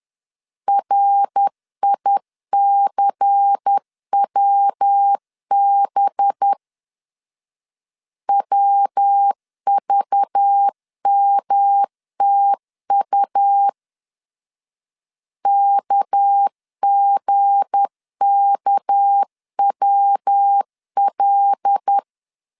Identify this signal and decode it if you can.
Morse code, heard as a keyed tone, about 10 wpm, 800 Hz. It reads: RICWB WVMTU KGKWL
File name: LEZIONI / 8 YJ45 / 8 Y 10wpm 2615